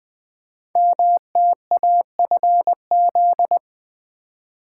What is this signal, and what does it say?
Morse code, heard as a keyed tone, about 20 wpm, 700 Hz. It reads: MTAFZ
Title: Code Groups - 5 Letters - 20wpm